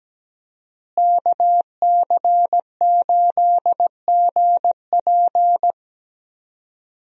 Morse code, heard as a keyed tone, about 17 wpm, 700 Hz. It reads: KC8GP
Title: USA Call Signs - 5 Characters - Repeated 3x - 17wpm